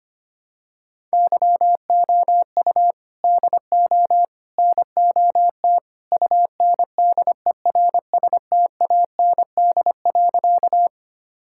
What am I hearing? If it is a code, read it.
Morse code, heard as a keyed tone, about 25 wpm, 700 Hz. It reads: YOU DO NOT UNDERSTAND.
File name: practice / 25wpm / Sentences with 3-Word Phrases - 25wpm